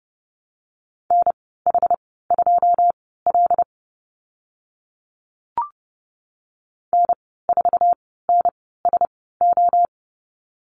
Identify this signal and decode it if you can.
Morse code, heard as a keyed tone, about 30 wpm, spaced as for 17 wpm, 700 Hz. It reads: NH2L N4NSO
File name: Lesson 14g - Number 2 - Call Signs 30wpm